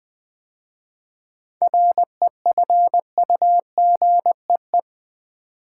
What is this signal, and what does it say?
Morse code, heard as a keyed tone, about 20 wpm, 700 Hz. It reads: REFUGEE
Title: 7 Letter Words - First 3 Letters Given - 20wpm